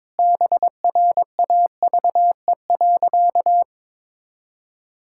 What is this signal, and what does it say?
Morse code, heard as a keyed tone, about 21 wpm, 700 Hz. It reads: BRAVE.
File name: Sentences with 9-Word Phrases - 22wpm